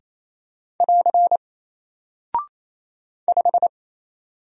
Morse code, heard as a keyed tone, about 28 wpm, 700 Hz. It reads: <AR> 5
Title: Single Character (Letter:Number:Punctuation:Prosign) Alt 28wpm